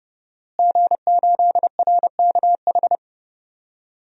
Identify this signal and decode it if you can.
Morse code, heard as a keyed tone, about 30 wpm, 700 Hz. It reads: G8RKH
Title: DXCC-100 Call Signs - 5 Characters - Repeated 2x - 30wpm